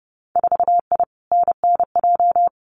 Morse code, heard as a keyed tone, about 30 wpm, 700 Hz. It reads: 4I NNJ
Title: Winter Field Day Exchanges 30wpm